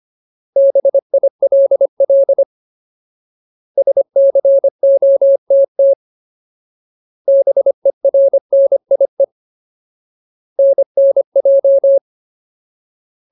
Morse code, heard as a keyed tone, about 25 wpm, 550 Hz. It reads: BILL SCOTT BERNIE NNJ
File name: Ultimate Word List / 25WPM Ultimate